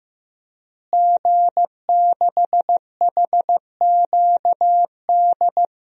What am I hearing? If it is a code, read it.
Morse code, heard as a keyed tone, about 15 wpm, 700 Hz. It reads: G6HQD
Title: DXCC-100 Call Signs - 5 Characters - Repeated 3x - 15wpm